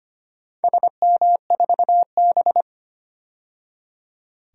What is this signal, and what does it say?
Morse code, heard as a keyed tone, about 25 wpm, 700 Hz. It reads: SM4B
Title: DXCC-100 Call Signs - 4 Characters - 25wpm